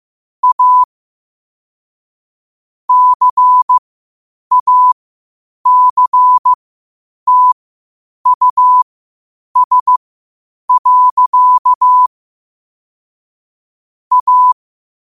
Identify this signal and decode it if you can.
Morse code, heard as a keyed tone, about 15 wpm, spaced as for 8 wpm, 1000 Hz. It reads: A CACTUS. A
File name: qotdfiles / 8wpm